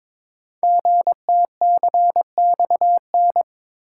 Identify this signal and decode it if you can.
Morse code, heard as a keyed tone, about 22 wpm, 700 Hz. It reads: GTCXN
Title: Code Groups - 5 Letters - 22wpm